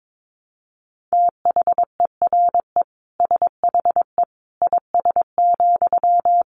Morse code, heard as a keyed tone, about 22 wpm, 700 Hz. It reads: THERE SHE IS,
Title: Sentences from Top 900 Words 22wpm